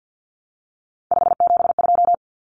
Morse code, heard as a keyed tone, about 48 wpm, 700 Hz. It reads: 572
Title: Common Cut Numbers - 3-Digit 50wpm